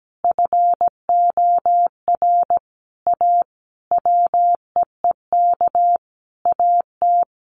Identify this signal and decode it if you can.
Morse code, heard as a keyed tone, about 17 wpm, 700 Hz. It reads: FOR A WEEK AT
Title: Sentences from Top 900 Words 17wpm